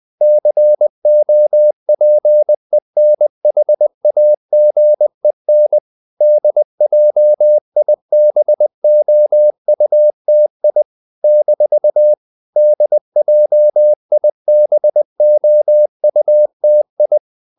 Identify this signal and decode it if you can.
Morse code, heard as a keyed tone, about 20 wpm, 600 Hz. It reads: COPENHAGEN DJIBOUTI - DJIBOUTI
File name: CW / Practice / NationalCapitals20